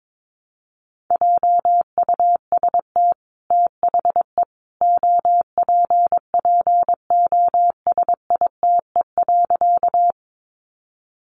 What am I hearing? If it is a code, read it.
Morse code, heard as a keyed tone, about 22 wpm, 700 Hz. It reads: JUST THE OPPOSITE.
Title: Sentences from Top 1300 Words 22wpm